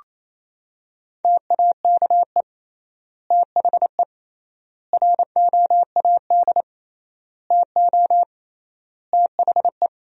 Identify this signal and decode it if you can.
Morse code, heard as a keyed tone, about 28 wpm, 700 Hz. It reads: TAKE THE ROAD TO THE
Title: 3x Word-Spacing - Sentences from Top 400 Words - 28wpm